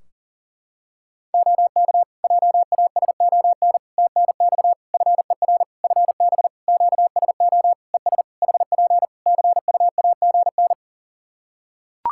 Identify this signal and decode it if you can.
Morse code, heard as a keyed tone, about 40 wpm, 700 Hz. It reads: OK JASON TNX FER FB QSO ES HP CUAGN
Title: Practice QSO - Exchange 4 - Ending - 40wpm